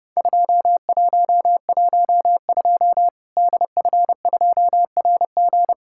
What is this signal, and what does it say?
Morse code, heard as a keyed tone, about 30 wpm, 700 Hz. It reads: 2112 DF2RG
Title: Worked All Europe DX Contest - QTC Exchange - 30wpm